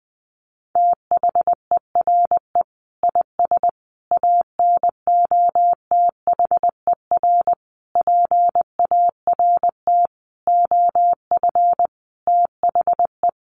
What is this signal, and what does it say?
Morse code, heard as a keyed tone, about 20 wpm, 700 Hz. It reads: THERE IS ANOTHER PART OF THE
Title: Sentences from Top 1300 Words 20wpm